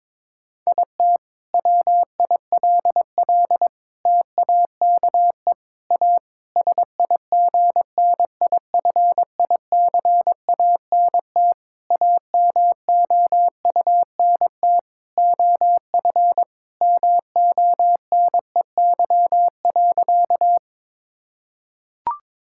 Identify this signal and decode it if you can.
Morse code, heard as a keyed tone, about 22 wpm, 700 Hz. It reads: IT WILL TAKE A SIGNIFICANT AMOUNT OF MONEY.
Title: Sentences from Top 1000 Words - Review Part 6 - 22wpm